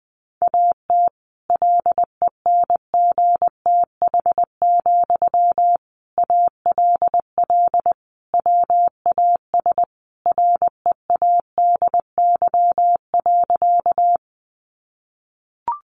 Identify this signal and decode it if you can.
Morse code, heard as a keyed tone, about 20 wpm, 700 Hz. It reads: AT LENGTH, ALL WAS READY.
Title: Sentences from Top 1000 Words 20wpm